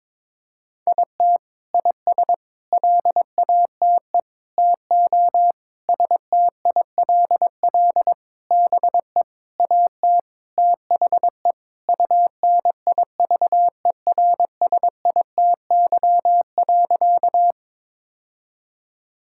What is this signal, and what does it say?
Morse code, heard as a keyed tone, about 22 wpm, 700 Hz. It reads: IT IS LATE TO STILL BE AT THE UNIVERSITY.
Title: Sentences from Top 1000 Words - Review Part 5 - 22wpm